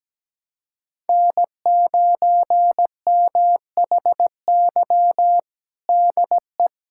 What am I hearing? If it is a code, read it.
Morse code, heard as a keyed tone, about 17 wpm, 700 Hz. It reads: N9MHY DE
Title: USA Callsign DE Callsign - 5 Characters - 17wpm